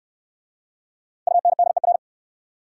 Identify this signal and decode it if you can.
Morse code, heard as a keyed tone, about 64 wpm, 700 Hz. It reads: UNDER